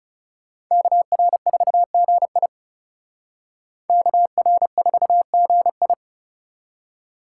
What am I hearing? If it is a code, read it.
Morse code, heard as a keyed tone, about 35 wpm, 700 Hz. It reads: KR4GI KR4GI
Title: Speed Racing - USA 2x2 Call Signs 30wpm